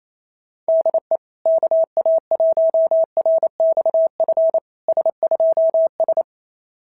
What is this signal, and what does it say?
Morse code, heard as a keyed tone, about 27 wpm, 660 Hz. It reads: DE KA1RXF S2S
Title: SOTA QSO Practice 28wpm